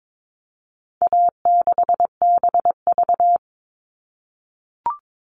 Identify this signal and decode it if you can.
Morse code, heard as a keyed tone, about 22 wpm, 700 Hz. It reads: A6BV
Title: All Cut Numbers - 4-Digit 22wpm